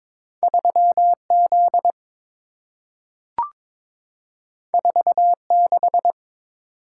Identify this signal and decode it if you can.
Morse code, heard as a keyed tone, about 22 wpm, 700 Hz. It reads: 3Z 46
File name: practice / 22wpm / Two Letter-Number Combinations - 22wpm